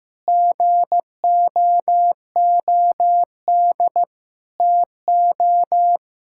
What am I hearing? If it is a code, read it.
Morse code, heard as a keyed tone, about 16 wpm, 700 Hz. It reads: GOOD TO
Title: Sentences from Top 1000 Words - Review Part 5 - 15wpm